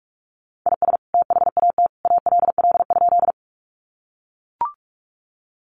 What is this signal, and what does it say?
Morse code, heard as a keyed tone, about 44 wpm, 700 Hz. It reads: IS THAT ALL?
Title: Sentences from Top 100 Words 45wpm